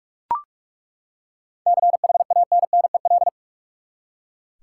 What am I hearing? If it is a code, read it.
Morse code, heard as a keyed tone, about 42 wpm, 700 Hz. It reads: CHANNEL
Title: Seven Letter Words - 45wpm